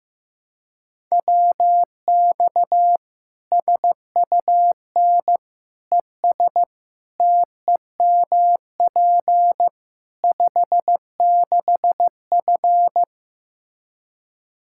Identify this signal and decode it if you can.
Morse code, heard as a keyed tone, about 15 wpm, 700 Hz. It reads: WX SUN ES TEMP 56F
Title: Practice QSO - Exchange 2 - Info - 15wpm